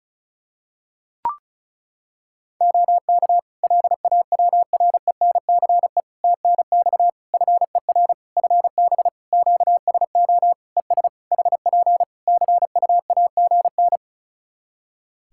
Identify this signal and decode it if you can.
Morse code, heard as a keyed tone, about 36 wpm, 700 Hz. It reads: OK LAWRENCE TNX FER FB QSO ES HP CUAGN
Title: Practice QSO - Exchange 4 - Ending - 35wpm